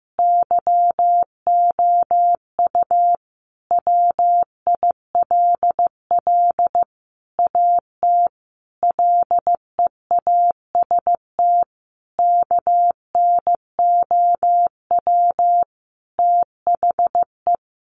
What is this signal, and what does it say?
Morse code, heard as a keyed tone, about 15 wpm, 700 Hz. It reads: YOU WILL AT LEAST KNOW THE
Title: Sentences from Top 1500 Words 15wpm